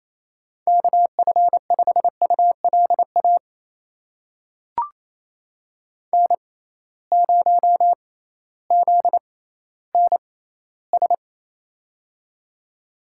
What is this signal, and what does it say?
Morse code, heard as a keyed tone, about 28 wpm, spaced as for 15 wpm, 700 Hz. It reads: KF5ULA N0ZNS
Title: USA Call Signs - 6x Character Spacing - 28wpm